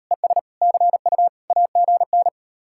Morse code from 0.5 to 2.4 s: CU AGN

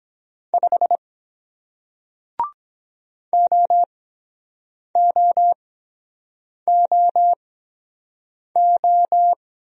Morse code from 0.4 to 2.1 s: 5

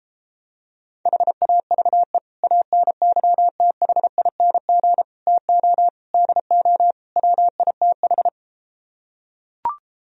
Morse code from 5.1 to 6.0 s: TO

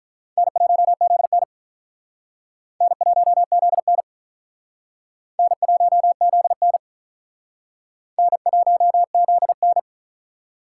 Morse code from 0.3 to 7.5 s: N1ZN N1ZN N1ZN

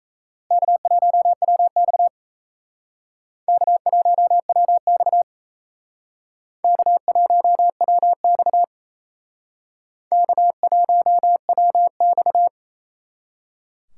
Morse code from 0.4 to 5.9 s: K1WX K1WX

